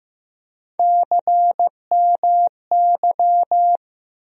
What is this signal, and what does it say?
Morse code, heard as a keyed tone, about 15 wpm, 700 Hz. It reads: CMY
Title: USA Call Sign Postfixes - 3 Letters 15wpm